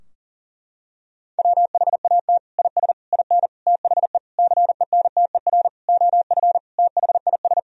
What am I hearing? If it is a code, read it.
Morse code, heard as a keyed tone, about 37 wpm, 700 Hz. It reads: WHAT IS IN THE CENTER OF THIS
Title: Sentences from Top 1000 Words - Review Part 2 - 40wpm